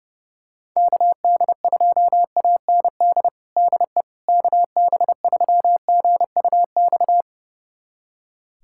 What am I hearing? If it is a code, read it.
Morse code, heard as a keyed tone, about 30 wpm, 700 Hz. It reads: KD2AND DE KB3GUX
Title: USA Callsign DE Callsign - 6 Characters - 30wpm